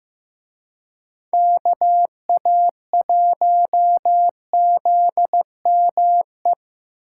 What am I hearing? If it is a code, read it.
Morse code, heard as a keyed tone, about 18 wpm, 700 Hz. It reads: KA1ZME
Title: Speed Racing - USA 2x3 Call Signs 15wpm